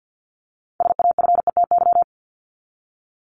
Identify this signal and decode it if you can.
Morse code, heard as a keyed tone, about 50 wpm, 700 Hz. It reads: SAFETY
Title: Top 501-750 Words 50wpm